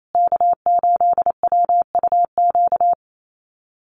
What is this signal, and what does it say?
Morse code, heard as a keyed tone, about 28 wpm, 700 Hz. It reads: K8WUQ
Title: USA Call Signs Repeated 3x 28wpm